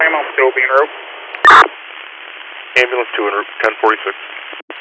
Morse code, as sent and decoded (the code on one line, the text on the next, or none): none